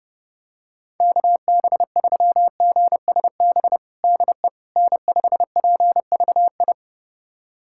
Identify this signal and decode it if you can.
Morse code, heard as a keyed tone, about 30 wpm, 700 Hz. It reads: KB3GSB DE N5PVI
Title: USA Callsign DE Callsign 30wpm